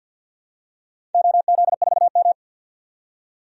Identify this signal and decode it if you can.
Morse code, heard as a keyed tone, about 46 wpm, 700 Hz. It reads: OZ4M
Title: DXCC-100 Call Signs - 4 Characters - 50wpm